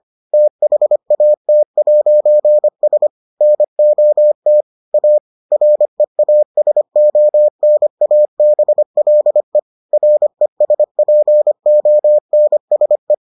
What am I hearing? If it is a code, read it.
Morse code, heard as a keyed tone, about 25 wpm, 600 Hz. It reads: THAT'S NOT A REASONABLE RESPONSE